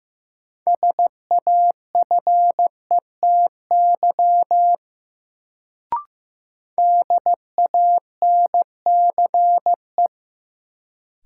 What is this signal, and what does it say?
Morse code, heard as a keyed tone, about 15 wpm, 700 Hz. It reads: SAFETY DANCE
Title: Sentences from Top 700 Words - New Word Drill - 15wpm